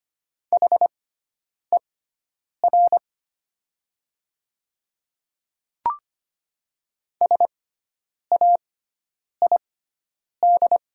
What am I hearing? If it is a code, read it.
Morse code, heard as a keyed tone, about 25 wpm, spaced as for 9 wpm, 700 Hz. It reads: HER SAID
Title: 6x Character Spacing - Top 500 Words - 25wpm